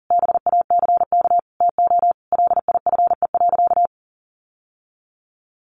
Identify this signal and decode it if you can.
Morse code, heard as a keyed tone, about 40 wpm, 700 Hz. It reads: BACK TO LIFE.
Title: Sentences from Top 600 Words 40wpm